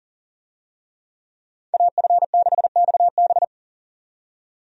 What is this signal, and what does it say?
Morse code, heard as a keyed tone, about 40 wpm, 700 Hz. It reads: AF6XB
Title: USA Call Signs - 5 Characters - 40wpm